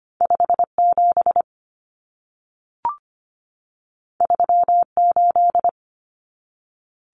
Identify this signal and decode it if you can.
Morse code, heard as a keyed tone, about 25 wpm, 700 Hz. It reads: 5 7 3 8
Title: Common Cut Numbers - 2-Digit 25wpm